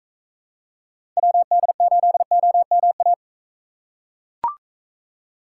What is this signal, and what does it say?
Morse code, heard as a keyed tone, about 42 wpm, 700 Hz. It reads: WD8OMA